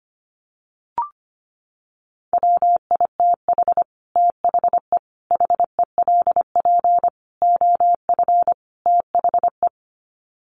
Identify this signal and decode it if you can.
Morse code, heard as a keyed tone, about 25 wpm, 700 Hz. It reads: WITH THE HELP OF THE
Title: Sets of 5 Words Encore 25wpm